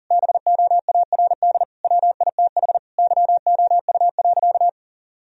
Morse code, heard as a keyed tone, about 40 wpm, 700 Hz. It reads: BOARD WITH YOU.
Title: Sentences from Top 1000 Words - Review Part 4 - 40wpm